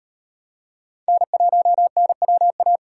Morse code, heard as a keyed tone, about 38 wpm, 700 Hz. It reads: N1NWA